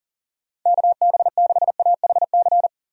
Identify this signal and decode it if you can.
Morse code, heard as a keyed tone, about 40 wpm, 700 Hz. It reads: KB6AHC
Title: USA Call Signs - 6 Characters - 40wpm